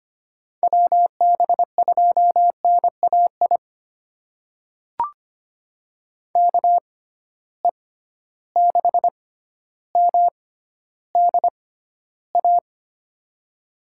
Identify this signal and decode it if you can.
Morse code, heard as a keyed tone, about 25 wpm, spaced as for 13 wpm, 700 Hz. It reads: WB2NAI KE6MDA
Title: USA Call Signs - 6 Characters - 6x Character Spacing - 25wpm